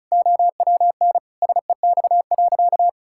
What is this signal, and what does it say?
Morse code, heard as a keyed tone, about 36 wpm, 700 Hz. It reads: OWN SEX.
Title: Sentences from Top 1600 Words 35wpm